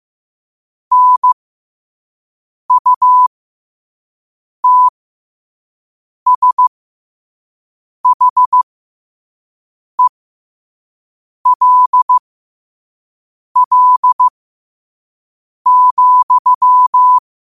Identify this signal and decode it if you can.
Morse code, heard as a keyed tone, about 15 wpm, spaced as for 5 wpm, 1000 Hz. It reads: NUTSHELL,
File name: qotdfiles / 5wpm